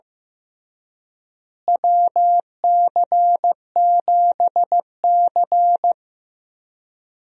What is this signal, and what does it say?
Morse code, heard as a keyed tone, about 15 wpm, 700 Hz. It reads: WC7C
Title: USA Call Signs - 4 Characters - Repeated 3x - 15wpm